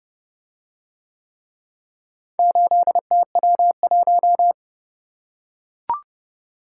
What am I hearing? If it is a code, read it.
Morse code, heard as a keyed tone, about 29 wpm, 700 Hz. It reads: 8TW1